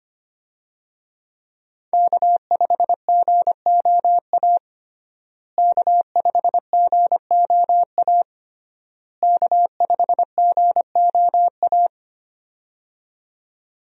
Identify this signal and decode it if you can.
Morse code, heard as a keyed tone, about 25 wpm, 700 Hz. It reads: K5GOA K5GOA K5GOA
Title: USA Call Signs - 5 Characters - Repeated 3x - 25wpm